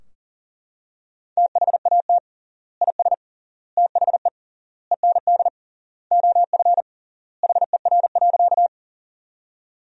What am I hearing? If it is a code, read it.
Morse code, heard as a keyed tone, about 40 wpm, 700 Hz. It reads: THAT IS THE END OF HER.
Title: 3x Word-Spacing - Sentences from Top 200 Words - 40wpm